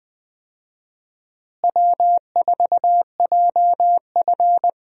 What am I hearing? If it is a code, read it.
Morse code, heard as a keyed tone, about 20 wpm, 700 Hz. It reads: W4JF
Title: USA Call Signs - 4 Characters - Repeated 3x - 20wpm